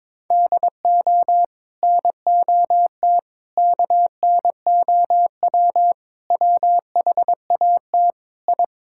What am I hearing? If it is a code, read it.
Morse code, heard as a keyed tone, about 22 wpm, 700 Hz. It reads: DO NOT KNOW WHAT I
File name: practice / 22wpm / Build A Phrase - 5-Word Phrases - 22wpm